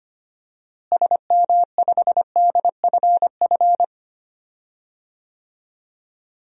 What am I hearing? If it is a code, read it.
Morse code, heard as a keyed tone, about 25 wpm, 700 Hz. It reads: SM5DFF
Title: DXCC-100 Call Signs - 6 Characters - Repeated 2x - 25wpm